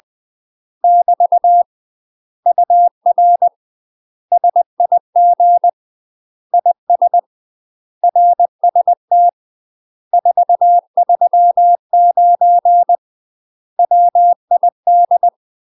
Morse code, heard as a keyed tone, about 20 wpm, 700 Hz. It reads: <BT> UR SIG IS RST 439 WID
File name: UK MASTER QSOs TRAINING / 20WPM / UK MASTER QSOs 0050